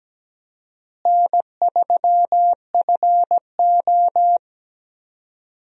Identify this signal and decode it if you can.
Morse code, heard as a keyed tone, about 17 wpm, 700 Hz. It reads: N3FO
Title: USA Call Signs - 4 Characters - Repeated 2x - 17wpm